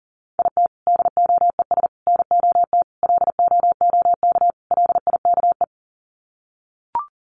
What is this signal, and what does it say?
Morse code, heard as a keyed tone, about 40 wpm, 700 Hz. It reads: IT DOES NOT LOOK LIKE
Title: Sets of 5 Words Encore2 40wpm